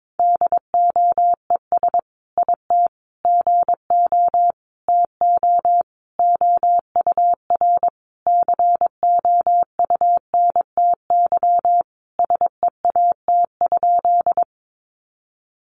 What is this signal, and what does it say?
Morse code, heard as a keyed tone, about 22 wpm, 700 Hz. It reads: DOES IT GO TO OUR COUNTY SEAT?
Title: Sentences from Top 1500 Words 22wpm